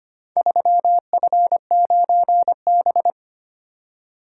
Morse code, heard as a keyed tone, about 25 wpm, 700 Hz. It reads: 3F9B
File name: practice / 25wpm / Four Letter-Number Combinations - 25wpm